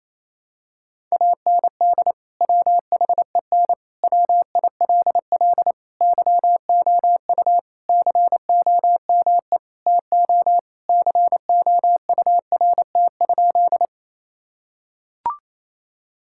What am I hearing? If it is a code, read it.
Morse code, heard as a keyed tone, about 28 wpm, 700 Hz. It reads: AND WHEN WILL YOU COME TO COURT?
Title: Sentences from Top 1600 Words 28wpm